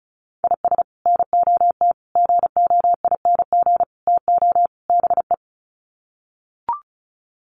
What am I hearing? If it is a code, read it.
Morse code, heard as a keyed tone, about 34 wpm, 700 Hz. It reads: IS NOT GOING TO BE
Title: Sets of 5 Words 35wpm